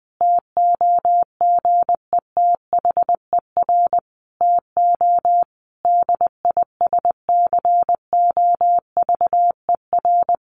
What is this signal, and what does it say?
Morse code, heard as a keyed tone, about 20 wpm, 700 Hz. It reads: TOGETHER TO DISCOVER
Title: Sentences from Top 1100 Words 20wpm